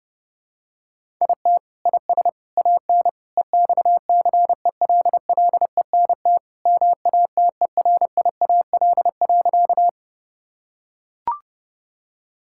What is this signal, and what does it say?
Morse code, heard as a keyed tone, about 30 wpm, 700 Hz. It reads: IT IS AN EXCELLENT MATERIAL.
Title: Sentences from Top 1000 Words - Review Part 5 - 30wpm